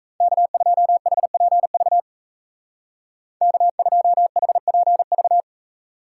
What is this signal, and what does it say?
Morse code, heard as a keyed tone, about 39 wpm, 700 Hz. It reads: K2HPV K2HPV